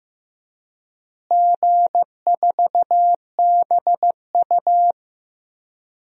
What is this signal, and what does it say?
Morse code, heard as a keyed tone, about 15 wpm, 700 Hz. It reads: G4BU
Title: DXCC-100 Call Signs - 4 Characters - Repeated 3x - 15wpm